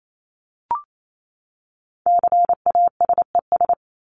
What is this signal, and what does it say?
Morse code, heard as a keyed tone, about 28 wpm, 700 Hz. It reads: CASES
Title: ES:S Suffix - Top 100 Words with Base Words 28wpm